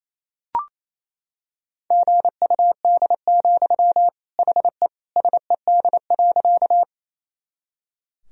Morse code, heard as a keyed tone, about 28 wpm, 700 Hz. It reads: GUD, HE SED.
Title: Sentences from Top 200 Words and Abbreviations 28wpm